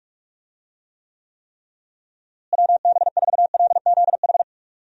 Morse code, heard as a keyed tone, about 46 wpm, 700 Hz. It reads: WB4LZH
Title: USA Call Signs - 6 Characters - Repeated 3x - 45wpm